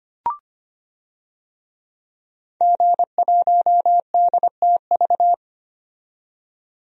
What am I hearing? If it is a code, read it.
Morse code, heard as a keyed tone, about 25 wpm, 700 Hz. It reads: G1DTV
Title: DXCC-100 Call Signs - 5 Characters - 25wpm